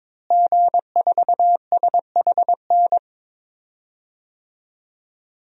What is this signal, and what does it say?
Morse code, heard as a keyed tone, about 22 wpm, 700 Hz. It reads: G4SHN
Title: DXCC-100 Call Signs - 5 Characters - Repeated 2x - 22wpm